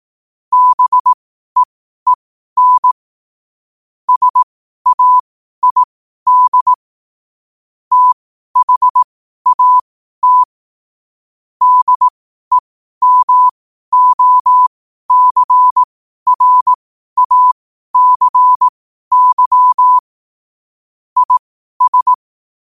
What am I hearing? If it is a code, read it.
Morse code, heard as a keyed tone, about 18 wpm, spaced as for 12 wpm, 1000 Hz. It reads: BEEN SAID THAT DEMOCRACY IS
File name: qotdfiles / 12wpm